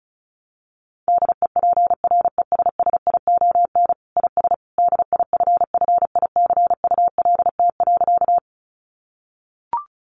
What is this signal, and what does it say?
Morse code, heard as a keyed tone, about 34 wpm, 700 Hz. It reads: DEPRESSION IS DIFFICULT.
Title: Sentences from Top 1300 Words 35wpm